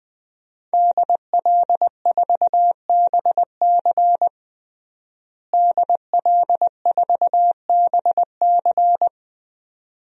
Morse code, heard as a keyed tone, about 21 wpm, 700 Hz. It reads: DL4BC DL4BC
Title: DXCC-100 Call Signs - 5 Characters - Repeated 2x - 20wpm